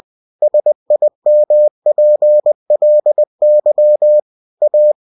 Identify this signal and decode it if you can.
Morse code, heard as a keyed tone, about 20 wpm, 600 Hz. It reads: SIMPLY A
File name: cw-books / winnie-the-pooh / winnie-20wpm-page-0079